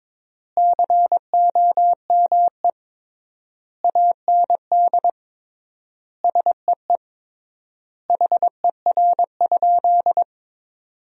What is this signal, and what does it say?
Morse code, heard as a keyed tone, about 22 wpm, 700 Hz. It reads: COME AND SEE HER?
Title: 3x Word-Spacing - Sentences from Top 100 Words - 22wpm